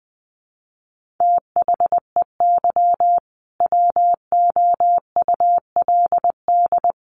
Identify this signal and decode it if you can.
Morse code, heard as a keyed tone, about 20 wpm, 700 Hz. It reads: THEY WOULD
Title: Sentences from Top 1400 Words 20wpm